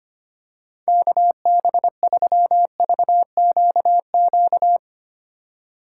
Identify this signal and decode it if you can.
Morse code, heard as a keyed tone, about 25 wpm, 700 Hz. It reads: KB3VQQ
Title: USA Call Signs - 6 Characters - 4x Character Spacing - 25wpm